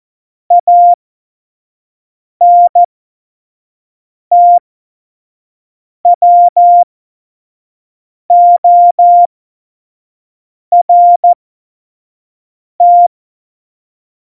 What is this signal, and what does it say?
Morse code, heard as a keyed tone, about 14 wpm, spaced as for 5 wpm, 700 Hz. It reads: ANTWORT
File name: Morsekurs / Witze / Witz 04. Dezember 2018, 25 F70